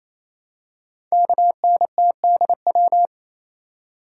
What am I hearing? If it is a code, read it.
Morse code, heard as a keyed tone, about 27 wpm, 700 Hz. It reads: KNTDW
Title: Code Groups - 5 Letters - 28wpm